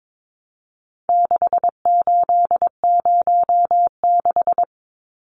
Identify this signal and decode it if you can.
Morse code, heard as a keyed tone, about 22 wpm, 700 Hz. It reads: 6806
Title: Numbers - 4-Digit 22wpm